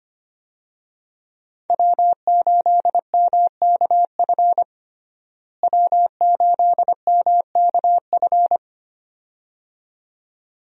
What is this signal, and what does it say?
Morse code, heard as a keyed tone, about 25 wpm, 700 Hz. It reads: W8MKF W8MKF
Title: USA Call Signs - 5 Characters - Repeated 2x - 25wpm